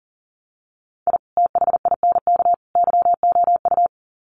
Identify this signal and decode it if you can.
Morse code, heard as a keyed tone, about 40 wpm, 700 Hz. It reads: I THINK YOU
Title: Sets of 3 Words 40wpm